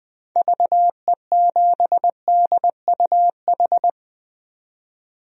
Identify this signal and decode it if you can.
Morse code, heard as a keyed tone, about 20 wpm, 700 Hz. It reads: VE7DUH
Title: DXCC-100 Call Signs - 6 Characters - 20wpm